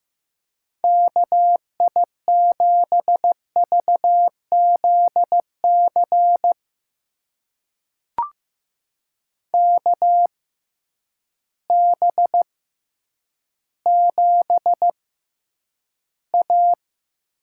Morse code, heard as a keyed tone, about 15 wpm, spaced as for 9 wpm, 700 Hz. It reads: KI7VZC KB7A
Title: USA Call Signs - 6 Characters - 6x Character Spacing - 15wpm